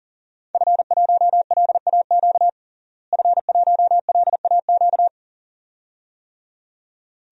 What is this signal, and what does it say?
Morse code, heard as a keyed tone, about 38 wpm, 700 Hz. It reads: F1LAQ F1LAQ
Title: DXCC-100 Call Signs - 5 Characters - Repeated 2x - 40wpm